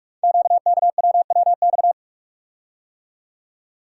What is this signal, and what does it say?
Morse code, heard as a keyed tone, about 44 wpm, 700 Hz. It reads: QKWWX
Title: Code Groups - 5 Letters - 45wpm